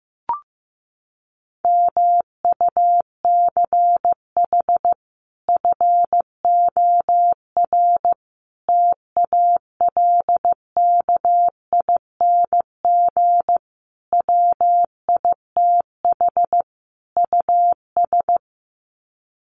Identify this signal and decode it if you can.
Morse code, heard as a keyed tone, about 15 wpm, 700 Hz. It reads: MUCH FOR TALKING WITH US
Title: Sets of 5 Words Encore 15wpm